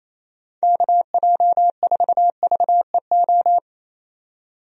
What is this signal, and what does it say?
Morse code, heard as a keyed tone, about 28 wpm, 700 Hz. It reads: KJ4VEO
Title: USA Call Signs - 6 Characters - 28wpm